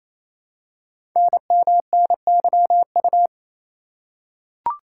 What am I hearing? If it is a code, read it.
Morse code, heard as a keyed tone, about 28 wpm, 700 Hz. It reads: NMNYU
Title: Code Groups - 5 Letters - 28wpm